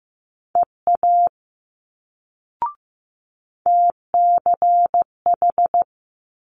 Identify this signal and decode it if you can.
Morse code, heard as a keyed tone, about 15 wpm, 700 Hz. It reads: EA TCH
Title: Multi-letter Phonemes Quick Fire 15wpm